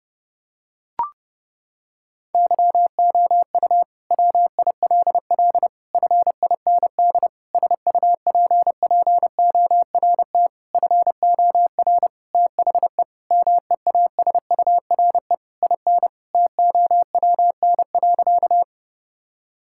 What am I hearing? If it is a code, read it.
Morse code, heard as a keyed tone, about 32 wpm, 700 Hz. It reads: YOU WILL FIND SUPPORT FOR THE MEASURE IN TOWN.
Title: Sentences from Top 1000 Words - Review Part 5 - 30wpm